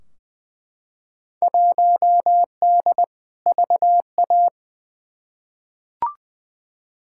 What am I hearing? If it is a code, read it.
Morse code, heard as a keyed tone, about 20 wpm, 700 Hz. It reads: 1D VA
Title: Field Day Exchanges - 20wpm